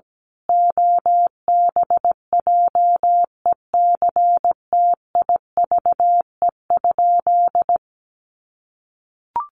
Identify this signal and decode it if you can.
Morse code, heard as a keyed tone, about 17 wpm, 700 Hz. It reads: OBJECTIVE?
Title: Sentences from Top 1200 Words 17wpm